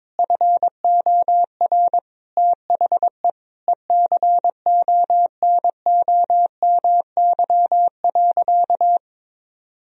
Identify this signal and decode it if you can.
Morse code, heard as a keyed tone, about 22 wpm, 700 Hz. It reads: FOR THE ECONOMY.
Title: Sentences from Top 1000 Words - Review Part 3 - 22wpm